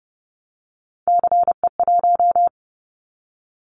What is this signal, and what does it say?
Morse code, heard as a keyed tone, about 30 wpm, 700 Hz. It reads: CE1
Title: Callsign Prefixes for Top 100 DXCC - 30wpm